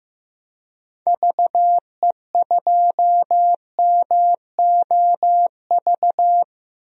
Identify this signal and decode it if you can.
Morse code, heard as a keyed tone, about 15 wpm, 700 Hz. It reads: VE2MOV
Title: DXCC-100 Call Signs - 6 Characters - Repeated 3x - 15wpm